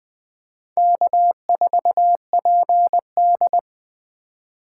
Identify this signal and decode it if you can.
Morse code, heard as a keyed tone, about 21 wpm, 700 Hz. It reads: K4PD
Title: USA Call Signs - 4 Characters - 2x Character Spacing - 20wpm